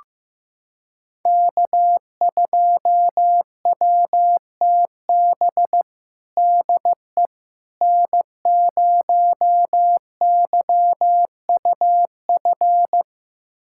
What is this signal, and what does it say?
Morse code, heard as a keyed tone, about 15 wpm, 700 Hz. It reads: K2WTB DE N0YUF
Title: USA Callsign DE Callsign - 5 Characters - 15wpm